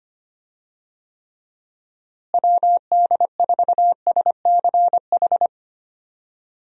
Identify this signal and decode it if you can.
Morse code, heard as a keyed tone, about 25 wpm, 700 Hz. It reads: WD4SCH
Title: USA Call Signs - 6 Characters - Repeated 2x - 25wpm